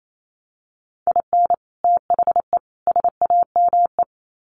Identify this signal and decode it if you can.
Morse code, heard as a keyed tone, about 28 wpm, 700 Hz. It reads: IN THE SAME
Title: Sets of 3 Words 28wpm